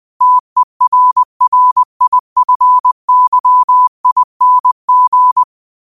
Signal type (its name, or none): Morse code